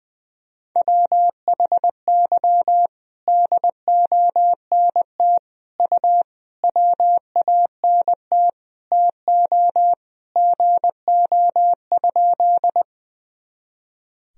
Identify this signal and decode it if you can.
Morse code, heard as a keyed tone, about 20 wpm, 700 Hz. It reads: WHY DONT U WANT TO GO?
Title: Sentences from Top 300 Words and Abbreviations 20wpm